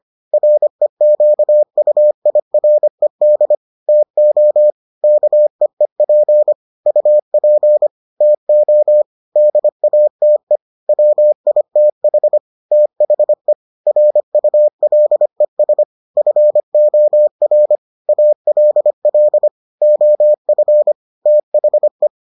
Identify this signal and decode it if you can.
Morse code, heard as a keyed tone, about 25 wpm, 600 Hz. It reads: REQUIRED TO KEEP UP TO DATE WITH THE RULES FOR ALL OF THE